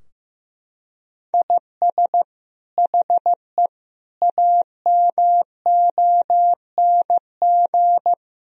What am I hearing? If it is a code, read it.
Morse code, heard as a keyed tone, about 15 wpm, 700 Hz. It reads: IS HE AMONG
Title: Sentences from Top 1000 Words - Review Part 3 - 15wpm